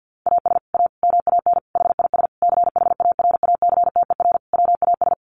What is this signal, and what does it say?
Morse code, heard as a keyed tone, about 50 wpm, 700 Hz. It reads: AS A MAN HIS CHARACTER WAS
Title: Sentences from Top 1200 Words 50wpm